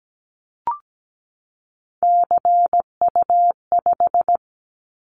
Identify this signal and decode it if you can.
Morse code, heard as a keyed tone, about 17 wpm, 700 Hz. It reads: CU5
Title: Callsign Prefixes for Top 50 DXCC - 17wpm